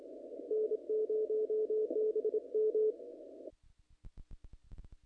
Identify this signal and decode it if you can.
Morse code, heard as a keyed tone, about 27 wpm, 440 Hz. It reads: N0BM
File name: contests / naqp-cw-k5tr-op-k5pi- -1900